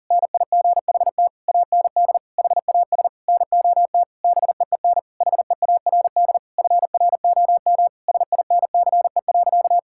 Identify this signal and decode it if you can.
Morse code, heard as a keyed tone, about 40 wpm, 700 Hz. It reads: NIGHT AND HAS NOT BEEN HEARD FROM SINCE.
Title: Sentences from Top 1600 Words 40wpm